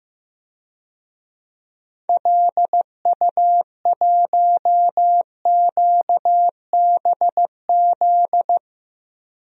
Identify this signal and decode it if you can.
Morse code, heard as a keyed tone, about 15 wpm, 700 Hz. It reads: LU1QBZ
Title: DXCC-100 Call Signs - 6 Characters - 15wpm